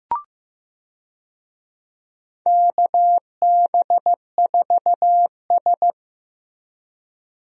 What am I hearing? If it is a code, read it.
Morse code, heard as a keyed tone, about 15 wpm, 700 Hz. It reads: KB4S
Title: USA Call Signs - 4 Characters - Repeated 2x - 15wpm